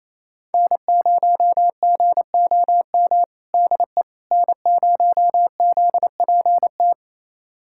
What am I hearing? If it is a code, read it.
Morse code, heard as a keyed tone, about 28 wpm, 700 Hz. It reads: N0GOM DE N0ZPT
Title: USA Callsign DE Callsign - 5 Characters - 28wpm